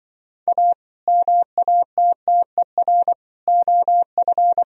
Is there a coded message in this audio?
Yes: Morse code from 0.5 to 4.6 s, heard as a keyed tone, about 26 wpm, 700 Hz. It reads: A MATTER OF